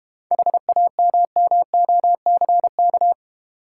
Morse code, heard as a keyed tone, about 32 wpm, 700 Hz. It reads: HAMMOCK